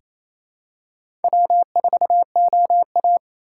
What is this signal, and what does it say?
Morse code, heard as a keyed tone, about 28 wpm, 700 Hz. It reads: W4OA
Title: USA Call Signs - 2x Character Spacing - 28wpm